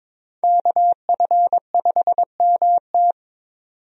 Morse code, heard as a keyed tone, about 22 wpm, 700 Hz. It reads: KF5MT
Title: USA Call Signs - 5 Characters - 22wpm